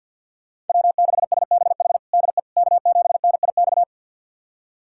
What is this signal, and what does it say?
Morse code, heard as a keyed tone, about 52 wpm, 700 Hz. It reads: W6SBH DE K7NIX
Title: USA Callsign DE Callsign 50wpm